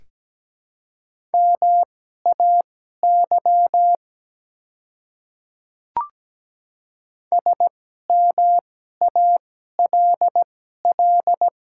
Morse code, heard as a keyed tone, about 17 wpm, spaced as for 12 wpm, 700 Hz. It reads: MAY SMALL
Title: 2x Character Spacing - Top 300 Words - 17wpm